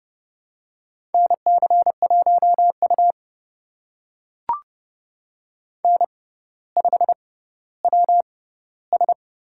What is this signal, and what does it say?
Morse code, heard as a keyed tone, about 30 wpm, spaced as for 15 wpm, 700 Hz. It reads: NC1U N5WS
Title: USA Call Signs - 6x Character Spacing - 30wpm